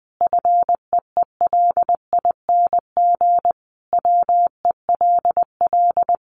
Morse code, heard as a keyed tone, about 20 wpm, 700 Hz. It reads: FEELING WELL
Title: Sentences from Top 1200 Words 20wpm